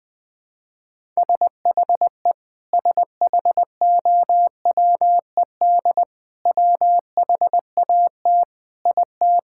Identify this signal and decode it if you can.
Morse code, heard as a keyed tone, about 19 wpm, 700 Hz. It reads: SHE SHOWED WHAT IT
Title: Sentences with 6-Word Phrases - 20wpm